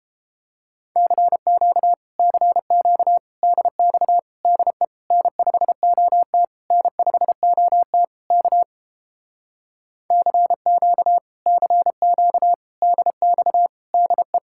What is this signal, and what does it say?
Morse code, heard as a keyed tone, about 29 wpm, 700 Hz. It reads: CQ CQ DX DE N5OT N5OT K CQ CQ DX DE